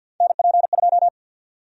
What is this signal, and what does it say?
Morse code, heard as a keyed tone, about 51 wpm, 700 Hz. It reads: NP2
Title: USA Call Sign Prefixes - Repeated 2x - 50wpm